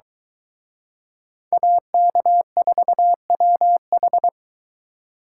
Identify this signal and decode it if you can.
Morse code, heard as a keyed tone, about 24 wpm, 700 Hz. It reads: AK4WH